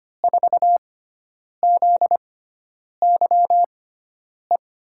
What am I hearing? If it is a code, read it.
Morse code, heard as a keyed tone, about 25 wpm, spaced as for 9 wpm, 700 Hz. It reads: 4ZYE
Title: USA Call Signs - 6 Characters - 6x Character Spacing - 25wpm